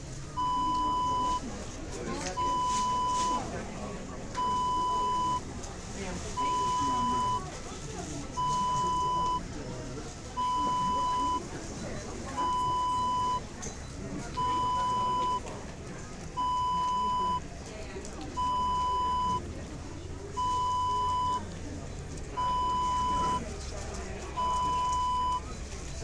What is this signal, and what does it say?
Speech. There is no code.